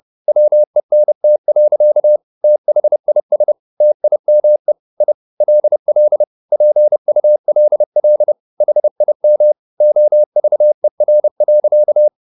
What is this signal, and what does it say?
Morse code, heard as a keyed tone, about 30 wpm, 600 Hz. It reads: WENT. THIS TIME I LL PULL HIM OVER.